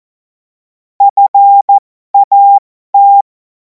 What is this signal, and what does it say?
Morse code, heard as a keyed tone, about 14 wpm, spaced as for 12 wpm, 800 Hz. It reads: FAT
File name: Morsekurs / Witze